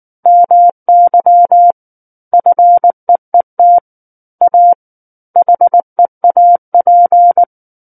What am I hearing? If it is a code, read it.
Morse code, heard as a keyed tone, about 19 wpm, 700 Hz. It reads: MY FEET A HEAP
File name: Book One / 10-Chapter 10 - 19 WPM